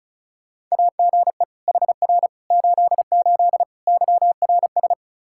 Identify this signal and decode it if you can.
Morse code, heard as a keyed tone, about 34 wpm, 700 Hz. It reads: AGE HR 88 YRS
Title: Practice QSO - Exchange 3 - More Info - 35wpm